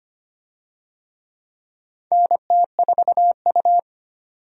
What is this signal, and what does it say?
Morse code, heard as a keyed tone, about 25 wpm, 700 Hz. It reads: NT4U
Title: USA Call Signs - 4 Characters - 25wpm